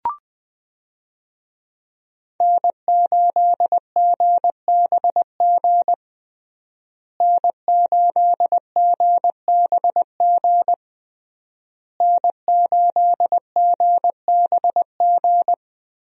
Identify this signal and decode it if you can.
Morse code, heard as a keyed tone, about 21 wpm, 700 Hz. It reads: N8GBG N8GBG N8GBG
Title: USA Call Signs - 5 Characters - Repeated 3x - 20wpm